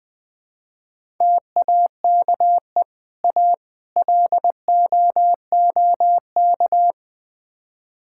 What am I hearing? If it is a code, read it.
Morse code, heard as a keyed tone, about 20 wpm, 700 Hz. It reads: TAKE A LOOK
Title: Build A Phrase - 4-Word Phrases - 20wpm